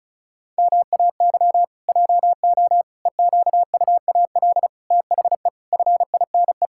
Morse code, heard as a keyed tone, about 36 wpm, 700 Hz. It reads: MAY JO EQUAL THE FINE
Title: Pangram Challenge v2 - 35wpm